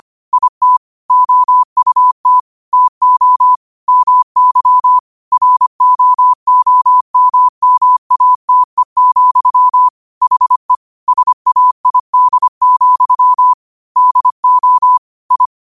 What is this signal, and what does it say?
Morse code, heard as a keyed tone, about 25 wpm, 1000 Hz. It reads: IT OUT TO MY ROOMMATE, HE SAID, DO I
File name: qotdfiles / 25wpm